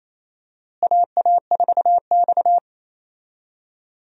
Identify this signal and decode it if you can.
Morse code, heard as a keyed tone, about 28 wpm, 700 Hz. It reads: AA4X
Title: USA Call Signs - 4 Characters - Repeated 3x - 28wpm